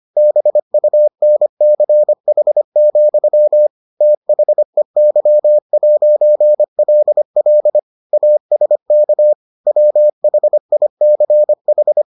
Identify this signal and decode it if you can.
Morse code, heard as a keyed tone, about 25 wpm, 600 Hz. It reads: BUNCH, THEY'LL ASK WHICH